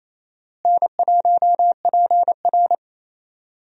Morse code, heard as a keyed tone, about 28 wpm, 700 Hz. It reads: N1PR
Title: USA Call Signs - 4 Characters - Repeated 2x - 28wpm